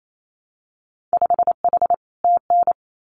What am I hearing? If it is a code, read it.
Morse code, heard as a keyed tone, about 28 wpm, 700 Hz. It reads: 5H TN
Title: Winter Field Day Exchanges Encore 28wpm